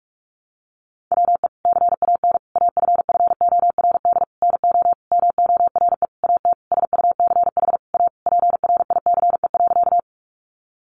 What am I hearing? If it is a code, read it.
Morse code, heard as a keyed tone, about 44 wpm, 700 Hz. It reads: WE CAN AFFORD NO MORE AT SUCH A PRICE.
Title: Sentences from Top 1100 Words 45wpm